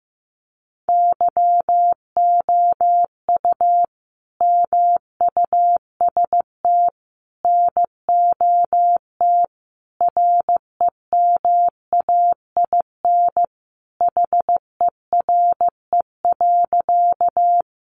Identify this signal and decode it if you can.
Morse code, heard as a keyed tone, about 15 wpm, 700 Hz. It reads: YOU MUST NOT REMAIN HERE.
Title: Sentences from Top 1200 Words 15wpm